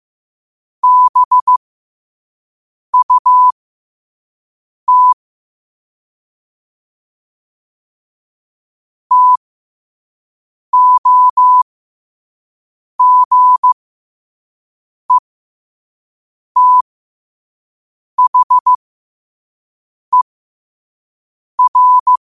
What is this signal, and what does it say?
Morse code, heard as a keyed tone, about 15 wpm, spaced as for 5 wpm, 1000 Hz. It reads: BUT TOGETHER
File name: qotdfiles / 5wpm